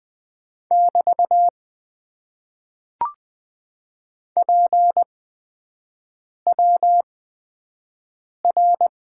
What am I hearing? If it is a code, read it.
Morse code, heard as a keyed tone, about 20 wpm, spaced as for 5 wpm, 700 Hz. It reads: <BT> PWR
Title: Top 90 QSO Elements - 8x Character Spacing - 20wpm